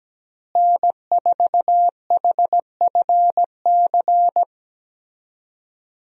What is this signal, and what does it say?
Morse code, heard as a keyed tone, about 17 wpm, 700 Hz. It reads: N4HFC
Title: USA Call Signs - 5 Characters - Repeated 3x - 17wpm